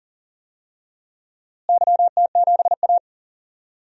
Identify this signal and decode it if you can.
Morse code, heard as a keyed tone, about 37 wpm, 700 Hz. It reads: YT7A
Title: DXCC-100 Call Signs - 4 Characters - Repeated 3x - 40wpm